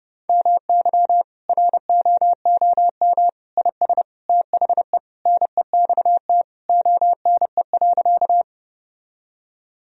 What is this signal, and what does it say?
Morse code, heard as a keyed tone, about 32 wpm, 700 Hz. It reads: MY ROOM IS THE NEXT ONE.
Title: Sentences from Top 1000 Words - Review Part 1 - 30wpm